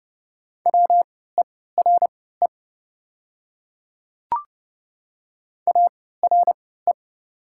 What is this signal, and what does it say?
Morse code, heard as a keyed tone, about 30 wpm, spaced as for 17 wpm, 700 Hz. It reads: WERE ARE
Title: 3x Character Spacing - Top 100 Words - 30wpm